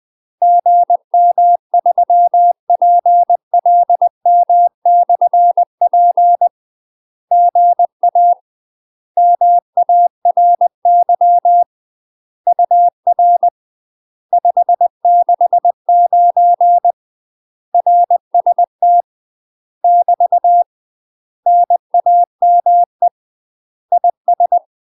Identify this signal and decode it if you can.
Morse code, heard as a keyed tone, about 20 wpm, 700 Hz. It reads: GM3PLM/P GA MARY UR 569 RST <BT> NAME IS
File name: UK MASTER QSOs TRAINING / 20WPM / UK MASTER QSOs 0012